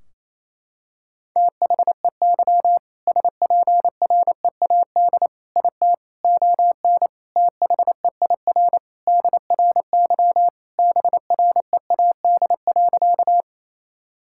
Morse code, heard as a keyed tone, about 28 wpm, 700 Hz. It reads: THEY SPREAD IT ON THEIR DRY BREAD.
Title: Sentences from Top 1600 Words 28wpm